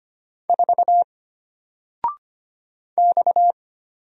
Morse code, heard as a keyed tone, about 25 wpm, 700 Hz. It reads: 4 X
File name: Single-Letter-Number-25wpm